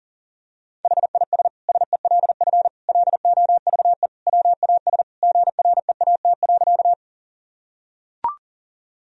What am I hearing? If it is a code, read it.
Morse code, heard as a keyed tone, about 37 wpm, 700 Hz. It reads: HIS SELF LOVE WAS GREAT.
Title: Sentences from Top 1000 Words - Review Part 1 - 40wpm